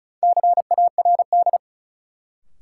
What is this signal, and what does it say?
Morse code, heard as a keyed tone, about 36 wpm, 700 Hz. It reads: CARD
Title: Sentences from Top 600 Words - New Word Drill - 35wpm